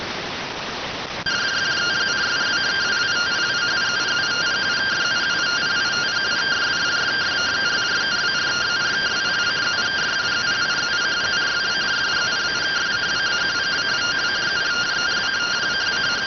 Two-tone teleprinter signal, about 170 Hz shift.